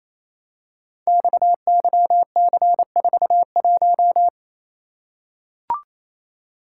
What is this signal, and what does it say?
Morse code, heard as a keyed tone, about 27 wpm, 700 Hz. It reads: XYC41
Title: Code Groups - 5 Letter:Numbers - 28wpm